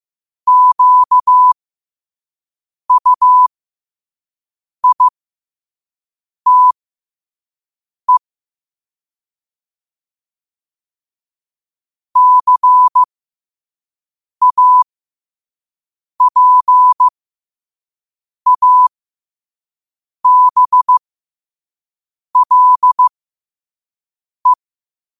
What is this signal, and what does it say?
Morse code, heard as a keyed tone, about 15 wpm, spaced as for 5 wpm, 1000 Hz. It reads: QUITE CAPABLE
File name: 5wpm